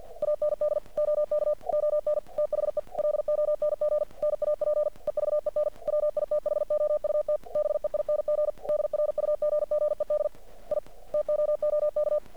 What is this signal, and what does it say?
Morse code, heard as a keyed tone, about 48 wpm, 600 Hz. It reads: AND OK ON THE LONG NAP EVEN WITHOUT BEING DRUGGED I TOOK
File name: archived / CRSnet